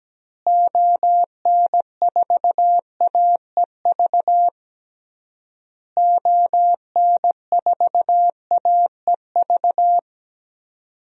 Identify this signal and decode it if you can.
Morse code, heard as a keyed tone, about 17 wpm, 700 Hz. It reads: ON4AEV ON4AEV
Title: DXCC-100 Call Signs - 6 Characters - Repeated 3x - 17wpm